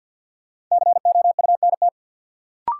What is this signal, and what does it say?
Morse code, heard as a keyed tone, about 53 wpm, 700 Hz. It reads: COUNT
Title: Sentences from Top 1500 Words - New Word Drill Alt - 50wpm